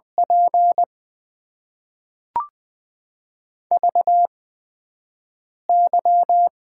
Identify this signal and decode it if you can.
Morse code, heard as a keyed tone, about 20 wpm, spaced as for 5 wpm, 700 Hz. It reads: P VY
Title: Top 90 QSO Elements - 8x Character Spacing - 20wpm